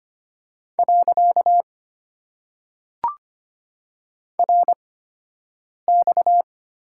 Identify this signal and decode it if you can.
Morse code, heard as a keyed tone, about 25 wpm, spaced as for 7 wpm, 700 Hz. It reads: . RX
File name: Top 90 QSO Elements - 8x Character Spacing - 25wpm